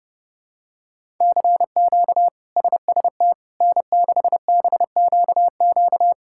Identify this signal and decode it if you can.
Morse code, heard as a keyed tone, about 30 wpm, 700 Hz. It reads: CQ SST N6BQQ
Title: K1USN SST QSO Practice - 30wpm